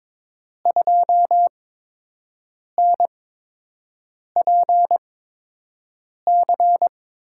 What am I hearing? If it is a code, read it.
Morse code, heard as a keyed tone, about 22 wpm, spaced as for 6 wpm, 700 Hz. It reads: 2NPC
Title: USA Call Signs - 6 Characters - 8x Character Spacing - 22wpm